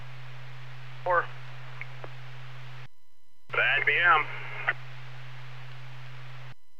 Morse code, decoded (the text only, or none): none